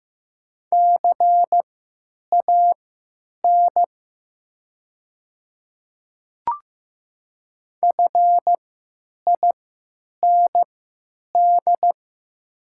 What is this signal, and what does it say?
Morse code, heard as a keyed tone, about 15 wpm, spaced as for 9 wpm, 700 Hz. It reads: CAN FIND
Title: 3x Character Spacing - Top 200 Words - 15wpm